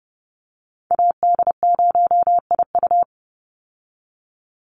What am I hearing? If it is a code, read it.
Morse code, heard as a keyed tone, about 30 wpm, 700 Hz. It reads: AD0IU
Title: Call Signs - 30wpm